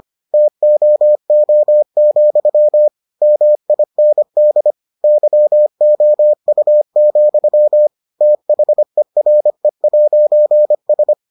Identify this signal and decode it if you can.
Morse code, heard as a keyed tone, about 25 wpm, 600 Hz. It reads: TOO, MIND YOU, THERE'S